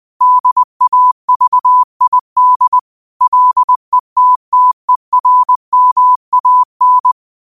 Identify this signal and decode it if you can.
Morse code, heard as a keyed tone, about 20 wpm, 1000 Hz. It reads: DAVID LETTERMAN
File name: qotdfiles / 20wpm